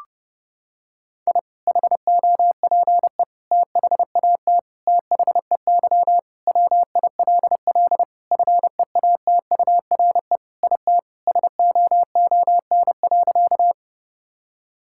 Morse code, heard as a keyed tone, about 32 wpm, 700 Hz. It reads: I HOPE THAT THEY WILL FEATURE IT SOON.
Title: Sentences from Top 1000 Words - Review Part 5 - 30wpm